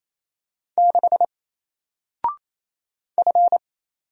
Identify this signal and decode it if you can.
Morse code, heard as a keyed tone, about 27 wpm, 700 Hz. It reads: 6 F
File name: Single-Letter-Number-28wpm